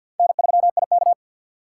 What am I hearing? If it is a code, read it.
Morse code, heard as a keyed tone, about 63 wpm, 700 Hz. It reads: N3IK